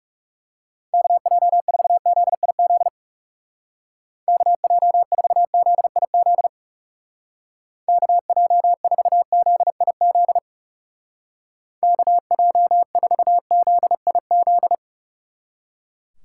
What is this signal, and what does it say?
Morse code, heard as a keyed tone, about 36 wpm, 700 Hz. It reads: KJ4ZIZ KJ4ZIZ KJ4ZIZ KJ4ZIZ
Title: Speed Racing - USA 2x3 Call Signs 30wpm